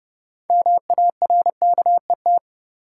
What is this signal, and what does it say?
Morse code, heard as a keyed tone, about 30 wpm, 700 Hz. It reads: MARKET
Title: Sentences from Top 600 Words - New Word Drill - 30wpm